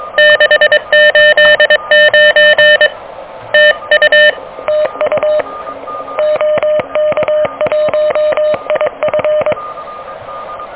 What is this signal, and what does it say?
Morse code, heard as a keyed tone, about 23 wpm, 600 Hz. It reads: 689 TU TU OK1IF
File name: OK QRP 2023 / OK QRP 2023